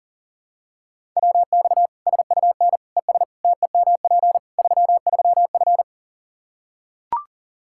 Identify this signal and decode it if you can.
Morse code, heard as a keyed tone, about 40 wpm, 700 Hz. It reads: WX SUN ES TEMP 33F
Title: K1USN SST QSO Practice 40wpm